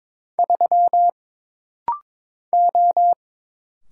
Morse code, heard as a keyed tone, about 22 wpm, 700 Hz. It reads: 3 O
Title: Single Letter-Number Alt - Rapid-Fire 22wpm